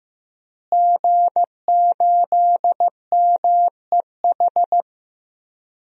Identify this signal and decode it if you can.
Morse code, heard as a keyed tone, about 15 wpm, 700 Hz. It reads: G8MEH
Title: DXCC-100 Call Signs - 5 Characters - Repeated 3x - 15wpm